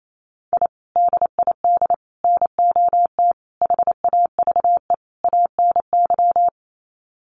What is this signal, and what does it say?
Morse code, heard as a keyed tone, about 28 wpm, 700 Hz. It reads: I DID NOT HAVE ANY
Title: Sets of 5 Words Encore 28wpm